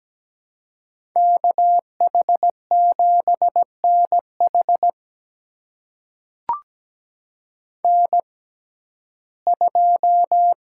Morse code, heard as a keyed tone, about 17 wpm, spaced as for 12 wpm, 700 Hz. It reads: KH7NH N2
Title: USA Call Signs - 5 Characters - 6x Character Spacing - 17wpm